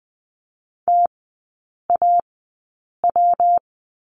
Morse code, heard as a keyed tone, about 20 wpm, spaced as for 8 wpm, 690 Hz. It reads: TAW
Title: 172 Lesson 28e - Number 9 - Three Characters 20wpm